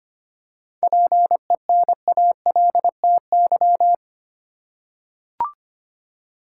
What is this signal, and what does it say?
Morse code, heard as a keyed tone, about 25 wpm, 700 Hz. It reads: PENALTY
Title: Seven Letter Words - 25wpm